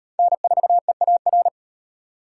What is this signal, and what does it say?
Morse code, heard as a keyed tone, about 38 wpm, 700 Hz. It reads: N4EAR